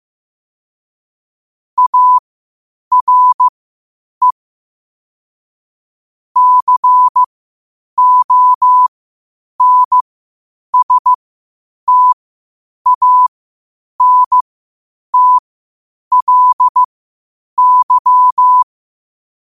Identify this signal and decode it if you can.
Morse code, heard as a keyed tone, about 15 wpm, spaced as for 8 wpm, 1000 Hz. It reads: ARE CONSTANTLY
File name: qotdfiles / 8wpm